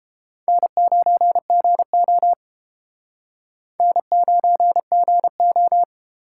Morse code, heard as a keyed tone, about 30 wpm, 700 Hz. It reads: N9GO N9GO